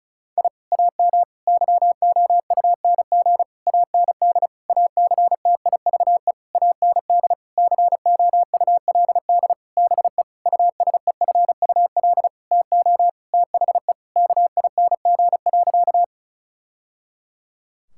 Morse code, heard as a keyed tone, about 36 wpm, 700 Hz. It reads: I AM YOUNG AND ACTIVE AND COULD BE USEFUL TO THE KING.
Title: Sentences from Top 1000 Words - Review Part 6 - 35wpm